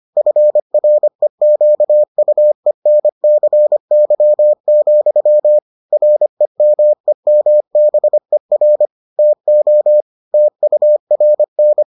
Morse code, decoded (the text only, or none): FREQUENCY, REMEMBER TO TURN